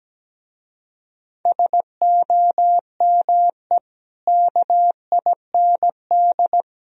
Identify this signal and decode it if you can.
Morse code, heard as a keyed tone, about 18 wpm, 700 Hz. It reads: SOME KIND
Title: Speed-Racing - Sets of 3 Words 17wpm